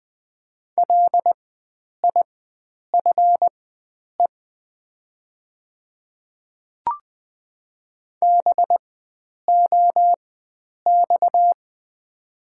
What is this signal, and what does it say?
Morse code, heard as a keyed tone, about 20 wpm, spaced as for 9 wpm, 700 Hz. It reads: LIFE BOX
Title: 4x Character Spacing - Top 500 Words - 20wpm